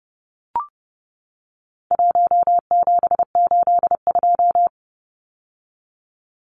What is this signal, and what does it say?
Morse code, heard as a keyed tone, about 30 wpm, 700 Hz. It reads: 1782
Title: Numbers - 4-Digit 30wpm